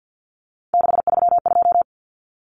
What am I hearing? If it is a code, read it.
Morse code, heard as a keyed tone, about 50 wpm, 700 Hz. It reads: 632
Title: Numbers - 3-Digit 50wpm